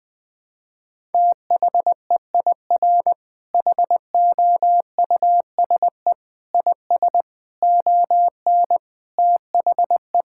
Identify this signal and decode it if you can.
Morse code, heard as a keyed tone, about 21 wpm, 700 Hz. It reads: THEIR HOUSE IS ON THE
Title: Sentences with 5-Word Phrases - 20wpm